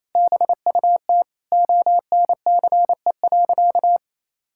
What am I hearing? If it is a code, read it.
Morse code, heard as a keyed tone, about 28 wpm, 700 Hz. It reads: BUT ONCE.
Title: Sentences from Top 1600 Words 28wpm